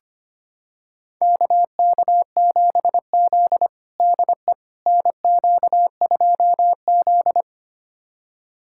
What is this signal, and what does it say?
Morse code, heard as a keyed tone, about 25 wpm, 700 Hz. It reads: KK7Z DE NQ2Z
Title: USA Callsign DE Callsign - 4 Characters - 25wpm